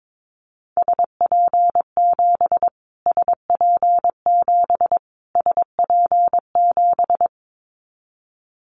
Morse code, heard as a keyed tone, about 22 wpm, 700 Hz. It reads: SP7 SP7 SP7
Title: Callsign Prefixes for Top 25 DXCC - Repeated 3x - 22wpm